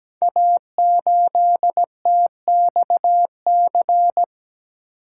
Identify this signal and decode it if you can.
Morse code, heard as a keyed tone, about 17 wpm, 690 Hz. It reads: A8TXC
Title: Code Groups - 5 Letter:Numbers - 17wpm